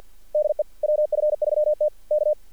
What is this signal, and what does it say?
Morse code, heard as a keyed tone, about 49 wpm, 600 Hz. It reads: DE WF3T K
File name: archived / CRSnet